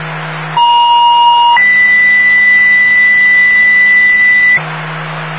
Two-tone page, heard as a one-second tone, then a three-second tone, about 950 then 2000 Hz. It is heard from 0.6 to 4.6 s.